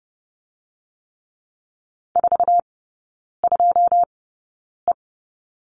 Morse code, heard as a keyed tone, about 30 wpm, spaced as for 9 wpm, 690 Hz. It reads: 42E
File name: Lesson 16e - Letter c - Three Characters 30wpm